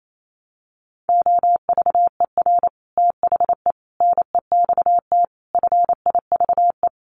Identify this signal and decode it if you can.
Morse code, heard as a keyed tone, about 28 wpm, 700 Hz. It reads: OVER THE NEXT FIVE
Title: Sets of 5 Words 28wpm